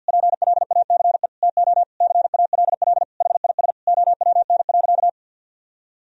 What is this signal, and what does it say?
Morse code, heard as a keyed tone, about 52 wpm, 700 Hz. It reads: PLACE TO CALL HIS OWN.
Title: Sentences from Top 1000 Words - Review Part 2 - 50wpm